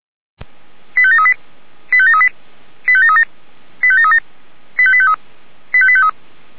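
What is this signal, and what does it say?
Five-tone selective call (ZVEI), heard as quick runs of five tones, tones of about 70 ms. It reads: 85738 85738 85737 85737 85853 85853
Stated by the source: Five-tone ZVEI selective call received by SDR.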